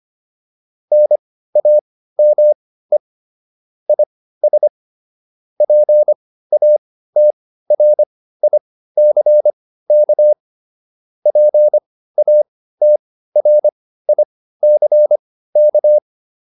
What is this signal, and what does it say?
Morse code, heard as a keyed tone, about 25 wpm, spaced as for 15 wpm, 600 Hz. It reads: NAME IS PATRICK PATRICK